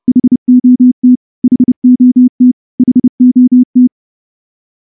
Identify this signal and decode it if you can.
Morse code, heard as a keyed tone, about 28 wpm, 250 Hz. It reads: HOT HOT HOT
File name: morse-training-30WPM